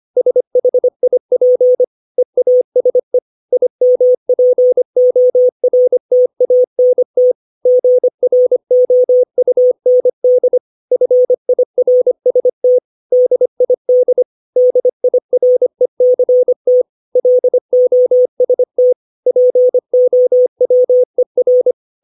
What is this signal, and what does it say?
Morse code, heard as a keyed tone, about 25 wpm, 500 Hz. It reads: SHIP EASE IMPORTANT GROUND FIRST DID DIRECT LOST POWER
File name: 632 words / 632 Words 25WPM